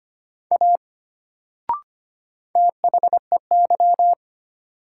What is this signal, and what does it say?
Morse code, heard as a keyed tone, about 25 wpm, 700 Hz. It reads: A THEY
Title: Sentences from Top 100 Words - New Word Drill - 25wpm